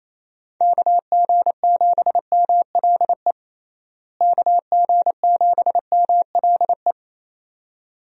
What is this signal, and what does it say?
Morse code, heard as a keyed tone, about 27 wpm, 700 Hz. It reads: KG7MLE KG7MLE
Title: USA Call Signs - 6 Characters - Repeated 2x - 28wpm